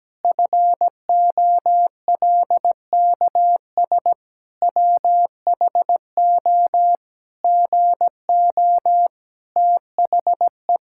Morse code, heard as a keyed tone, about 17 wpm, 700 Hz. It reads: FOLKS WHO GO THE
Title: Pangram Challenge v2 - 17wpm